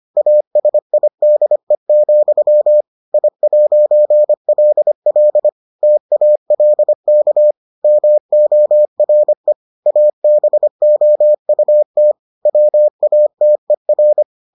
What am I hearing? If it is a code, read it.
Morse code, heard as a keyed tone, about 25 wpm, 600 Hz. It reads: ASIDE, I'LL TALK MORE ABOUT WATER